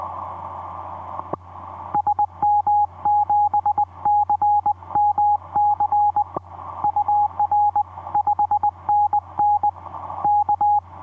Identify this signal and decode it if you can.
Morse code, heard as a keyed tone, about 20 wpm, 840 Hz. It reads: SM7CMC UR 5NN K
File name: v5-dk1ce dm1tde etc 17m cw 28dec2010 0942